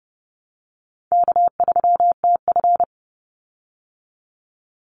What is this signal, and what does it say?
Morse code, heard as a keyed tone, about 30 wpm, 700 Hz. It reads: K3TF
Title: Easy Call Signs Repeated 3x - 30wpm